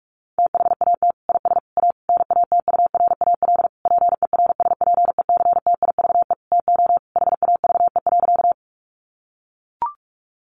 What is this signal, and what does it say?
Morse code, heard as a keyed tone, about 44 wpm, 700 Hz. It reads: THAT IS A NATURAL PERSPECTIVE TO HAVE.
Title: Sentences from Top 1200 Words 45wpm